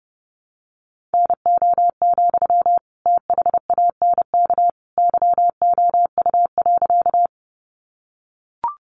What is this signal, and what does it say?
Morse code, heard as a keyed tone, about 28 wpm, 700 Hz. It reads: NO, THANK YOU.
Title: Sentences from Top 600 Words 30wpm